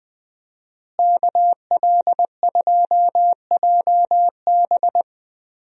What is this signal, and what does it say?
Morse code, heard as a keyed tone, about 20 wpm, 700 Hz. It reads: KL2JB
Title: USA Call Signs - 5 Characters - Repeated 3x - 20wpm